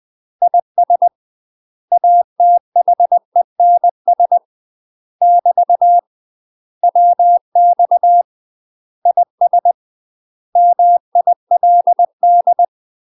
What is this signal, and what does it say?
Morse code, heard as a keyed tone, about 20 wpm, 700 Hz. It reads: IS ATHENS <BT> WX IS MILD
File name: UK MASTER QSOs TRAINING / 20WPM / UK MASTER QSOs 0044